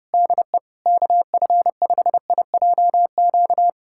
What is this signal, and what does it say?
Morse code, heard as a keyed tone, about 30 wpm, 700 Hz. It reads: DE KF5IJQ
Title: USA Callsign DE Callsign - 6 Characters - 30wpm